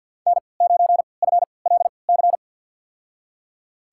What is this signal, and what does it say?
Morse code, heard as a keyed tone, about 51 wpm, spaced as for 28 wpm, 700 Hz. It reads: N9FLC
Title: USA Call Signs - 5 Characters - 3x Character Spacing - 50wpm